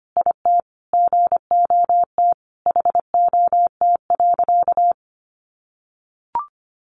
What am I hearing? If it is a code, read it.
Morse code, heard as a keyed tone, about 25 wpm, 700 Hz. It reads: IT GOT HOT.
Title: Sentences from Top 1200 Words 25wpm